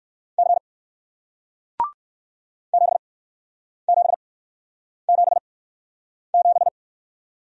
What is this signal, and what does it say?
Morse code, heard as a keyed tone, about 60 wpm, spaced as for 9 wpm, 700 Hz. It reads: C 7777